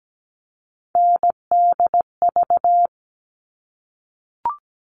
Morse code, heard as a keyed tone, about 17 wpm, 700 Hz. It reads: NDV
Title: All Cut Numbers - 3-Digit 17wpm